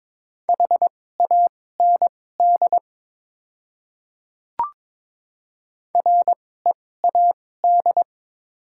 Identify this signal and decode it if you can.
Morse code, heard as a keyed tone, about 22 wpm, spaced as for 16 wpm, 690 Hz. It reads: HAND READ
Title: 2x Character Spacing - Top 200 Words - 22wpm